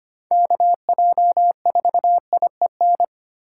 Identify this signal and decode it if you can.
Morse code, heard as a keyed tone, about 33 wpm, 700 Hz. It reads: KJ4IEN